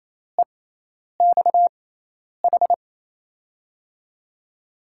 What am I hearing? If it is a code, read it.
Morse code, heard as a keyed tone, about 28 wpm, spaced as for 10 wpm, 700 Hz. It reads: EXH
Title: USA Call Signs - 5 Characters - 6x Character Spacing - 28wpm